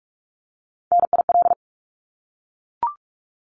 Morse code, heard as a keyed tone, about 44 wpm, 700 Hz. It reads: NIL
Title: Abbreviations Repeated 3x - 45wpm